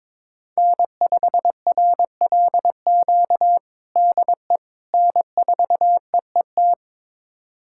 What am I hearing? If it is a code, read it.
Morse code, heard as a keyed tone, about 21 wpm, 700 Hz. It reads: N5RLQ DE N4EET
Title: USA Callsign DE Callsign - 5 Characters - 22wpm